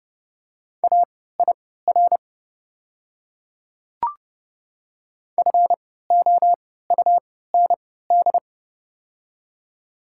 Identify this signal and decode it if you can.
Morse code, heard as a keyed tone, about 30 wpm, spaced as for 17 wpm, 700 Hz. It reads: AIR FOUND
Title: 3x Character Spacing - Top 300 Words - 30wpm